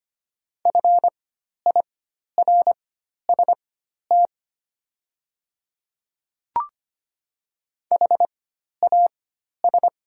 Morse code, heard as a keyed tone, about 25 wpm, spaced as for 12 wpm, 690 Hz. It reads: FIRST HAS
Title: 4x Character Spacing - Top 200 Words - 25wpm